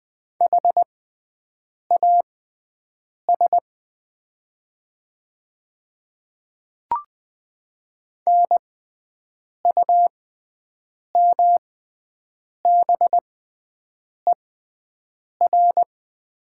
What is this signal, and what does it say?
Morse code, heard as a keyed tone, about 20 wpm, spaced as for 7 wpm, 700 Hz. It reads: HAS NUMBER
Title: 6x Character Spacing - Top 100 Words - 20wpm